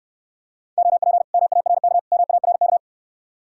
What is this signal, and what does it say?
Morse code, heard as a keyed tone, about 69 wpm, 700 Hz. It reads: OP MARK MARK